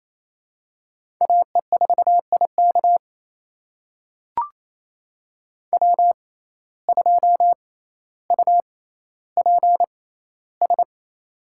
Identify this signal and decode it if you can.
Morse code, heard as a keyed tone, about 28 wpm, spaced as for 14 wpm, 700 Hz. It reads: AE4IK W2UPS
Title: USA Call Signs - 6x Character Spacing - 28wpm